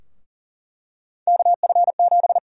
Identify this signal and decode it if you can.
Morse code, heard as a keyed tone, about 40 wpm, 700 Hz. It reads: KF7
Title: USA Call Sign Prefixes - 40wpm